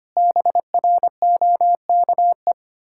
Morse code, heard as a keyed tone, about 25 wpm, 700 Hz. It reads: BROKE
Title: Pangram Challenge v2 - 25wpm